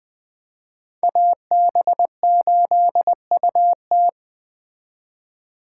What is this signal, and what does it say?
Morse code, heard as a keyed tone, about 20 wpm, 700 Hz. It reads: AB8UT
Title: Speed Racing - USA Call Signs 17wpm